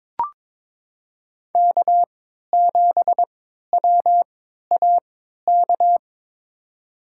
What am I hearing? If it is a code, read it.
Morse code, heard as a keyed tone, about 22 wpm, spaced as for 13 wpm, 700 Hz. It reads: K7WAK
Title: USA Call Signs - 5 Characters - 3x Character Spacing - 22wpm